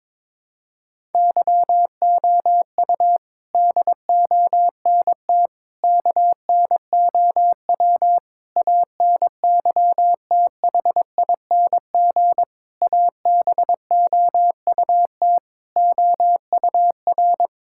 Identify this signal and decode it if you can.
Morse code, heard as a keyed tone, about 22 wpm, 700 Hz. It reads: YOU DONT KNOW ANYTHING ABOUT OUR
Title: Sentences from Top 1000 Words - Review Part 6 - 22wpm